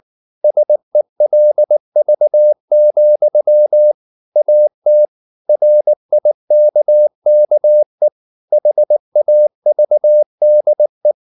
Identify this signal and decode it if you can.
Morse code, heard as a keyed tone, about 19 wpm, 600 Hz. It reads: SELV, AT RIKKE HAVDE
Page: Ferie 19 wpm